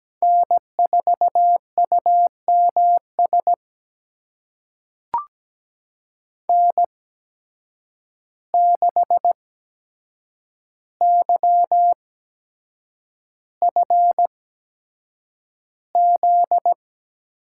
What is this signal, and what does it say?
Morse code, heard as a keyed tone, about 17 wpm, spaced as for 7 wpm, 700 Hz. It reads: N4UMS N6YFZ
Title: USA Call Signs - 8x Character Spacing - 17wpm